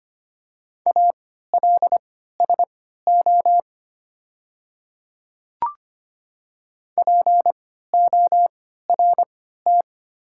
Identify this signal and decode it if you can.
Morse code, heard as a keyed tone, about 25 wpm, spaced as for 14 wpm, 700 Hz. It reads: ALSO PORT
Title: 3x Character Spacing - Top 200 Words - 25wpm